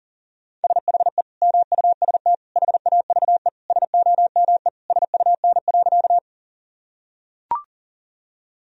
Morse code, heard as a keyed tone, about 40 wpm, 700 Hz. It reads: SHE MUST HAVE SOME SUN.
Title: Sentences from Top 1000 Words - Review Part 1 - 40wpm